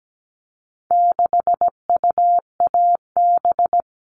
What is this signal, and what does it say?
Morse code, heard as a keyed tone, about 17 wpm, 700 Hz. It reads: 6UAB
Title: All Cut Numbers - 4-Digit 17wpm